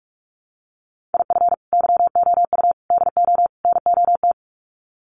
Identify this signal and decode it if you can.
Morse code, heard as a keyed tone, about 44 wpm, 700 Hz. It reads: IF YOU DO NOT
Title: Sets of 4 Words 45wpm